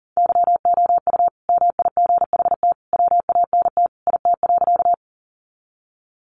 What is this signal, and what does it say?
Morse code, heard as a keyed tone, about 40 wpm, 700 Hz. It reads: YOU MIGHT WANT IT.
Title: Sentences from Top 300 Words 40wpm